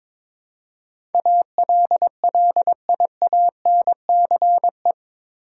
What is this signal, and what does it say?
Morse code, heard as a keyed tone, about 22 wpm, 700 Hz. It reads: ALLIANCE
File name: practice / 22wpm / Eight Letter Words - 22wpm